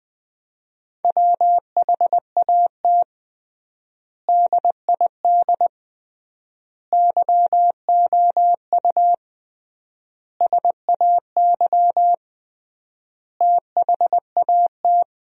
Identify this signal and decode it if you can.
Morse code, heard as a keyed tone, about 20 wpm, 700 Hz. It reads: WHAT DID YOU SAY THAT
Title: 3x Word-Spacing - Sentences from Top 200 Words - 20wpm